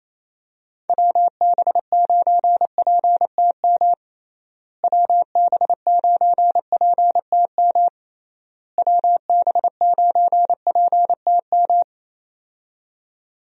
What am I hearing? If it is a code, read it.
Morse code, heard as a keyed tone, about 28 wpm, 700 Hz. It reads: WB9PTM WB9PTM WB9PTM
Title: USA Call Signs - 6 Characters - Repeated 3x - 28wpm